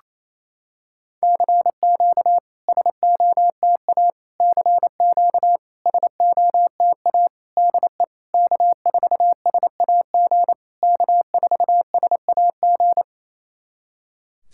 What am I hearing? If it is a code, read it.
Morse code, heard as a keyed tone, about 29 wpm, 690 Hz. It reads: CQ SOTA CQ SOTA DE K4SAG K4SAG
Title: SOTA QSO Practice 28wpm